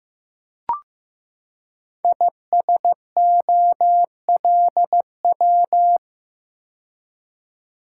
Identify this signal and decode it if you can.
Morse code, heard as a keyed tone, about 15 wpm, 700 Hz. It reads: ISOLW
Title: Code Groups - 5 Letter:Numbers - 15wpm